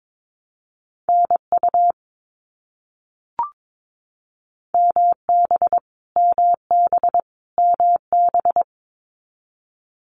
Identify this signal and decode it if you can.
Morse code, heard as a keyed tone, about 22 wpm, 700 Hz. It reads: NU MB MB MB
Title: Canadian Provinces and Territories - Repeated 3x - 22wpm